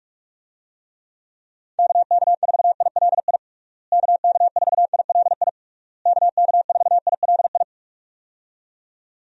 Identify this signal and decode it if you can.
Morse code, heard as a keyed tone, about 46 wpm, 700 Hz. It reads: KK4ILI KK4ILI KK4ILI
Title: USA Call Signs - 6 Characters - Repeated 3x - 45wpm